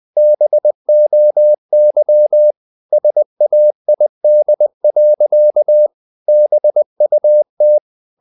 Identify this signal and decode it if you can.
Morse code, heard as a keyed tone, about 20 wpm, 600 Hz. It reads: BOY SAID. BUT
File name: the old man and the sea / the old man and the sea part 1 20wpm